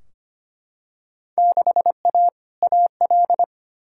Morse code, heard as a keyed tone, about 25 wpm, 700 Hz. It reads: 6A AL
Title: Field Day Exchanges Encore - 25wpm